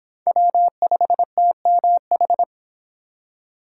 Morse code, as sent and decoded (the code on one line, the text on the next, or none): .-- ..... - -- ....
W5TMH